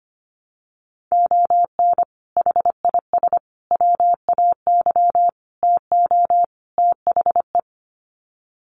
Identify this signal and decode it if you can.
Morse code, heard as a keyed tone, about 25 wpm, 700 Hz. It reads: ON HIS WAY TO THE
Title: Sets of 5 Words Encore2 25wpm